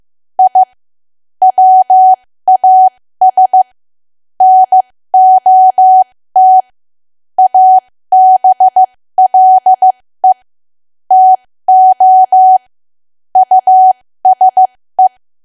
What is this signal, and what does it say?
Morse code, heard as a keyed tone, about 15 wpm, spaced as for 13 wpm, 750 Hz. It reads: I WAS NOT ABLE TO USE
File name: cwa / W1AW-Level2 / 13WPM